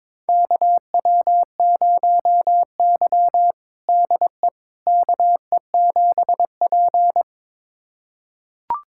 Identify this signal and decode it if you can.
Morse code, heard as a keyed tone, about 22 wpm, 700 Hz. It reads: KW0Y DE KE7P
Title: USA Callsign DE Callsign - 4 Characters - 22wpm